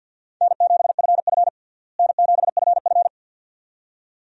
Morse code, heard as a keyed tone, about 50 wpm, 700 Hz. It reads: N7FF N7FF
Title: USA Call Signs - 4 Characters - Repeated 2x - 50wpm